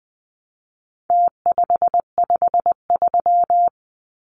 Morse code, heard as a keyed tone, about 20 wpm, 700 Hz. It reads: T553
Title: Numbers - Contest Style - 4-Digit 20wpm